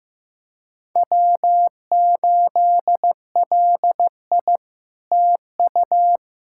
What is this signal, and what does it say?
Morse code, heard as a keyed tone, about 15 wpm, 700 Hz. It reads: W8LI TU
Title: SOTA QSO Practice 15wpm